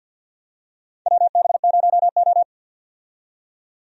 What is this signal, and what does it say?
Morse code, heard as a keyed tone, about 52 wpm, 700 Hz. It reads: WB0O
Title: USA Call Signs - 4 Characters - 50wpm